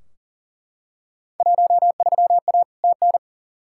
Morse code, heard as a keyed tone, about 40 wpm, 700 Hz. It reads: 13A TN
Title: Field Day Exchanges Encore - 40wpm